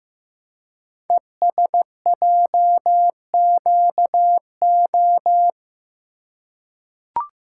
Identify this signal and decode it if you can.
Morse code, heard as a keyed tone, about 15 wpm, 700 Hz. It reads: ESJQO
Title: Code Groups - 5 Letters - 15wpm